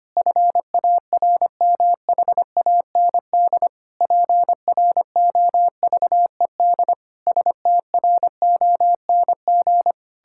Morse code, heard as a keyed tone, about 25 wpm, 700 Hz. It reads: FARMHAND PROVED STRONG
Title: Pangram Challenge v2 - 25wpm